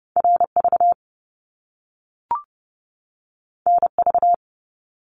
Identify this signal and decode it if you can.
Morse code, heard as a keyed tone, about 30 wpm, 700 Hz. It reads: RV NV
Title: Top Two Letter Combinations in Words 30wpm